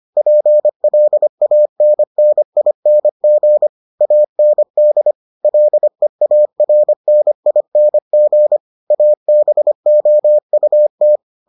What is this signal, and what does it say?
Morse code, heard as a keyed tone, about 25 wpm, 600 Hz. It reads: PLANNING AND LEARNING ABOUT